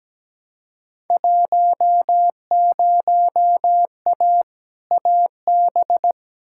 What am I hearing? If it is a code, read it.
Morse code, heard as a keyed tone, about 17 wpm, 700 Hz. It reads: 10A AB
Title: Field Day Exchanges Encore - 17wpm